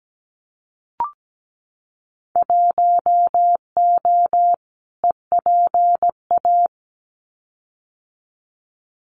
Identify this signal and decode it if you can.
Morse code, heard as a keyed tone, about 17 wpm, 700 Hz. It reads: 1O EPA
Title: Winter Field Day Exchanges 17wpm